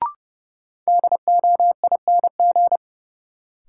Morse code, heard as a keyed tone, about 32 wpm, 700 Hz. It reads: DOING
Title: Sets of 4 Words - New Word Drill - 30wpm